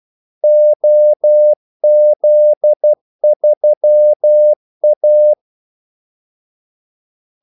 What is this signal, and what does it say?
Morse code, heard as a keyed tone, about 12 wpm, 600 Hz. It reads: OZ3A
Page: 2 ekstra ordmellemrum mellem kaldesignalerne